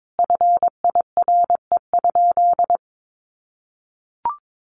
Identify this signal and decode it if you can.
Morse code, heard as a keyed tone, about 22 wpm, 700 Hz. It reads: FIRE?
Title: Sentences from Top 1000 Words 22wpm